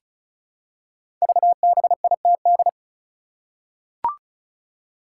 Morse code, heard as a keyed tone, about 36 wpm, 700 Hz. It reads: VBITD
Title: Code Groups - 5 Letters - 35wpm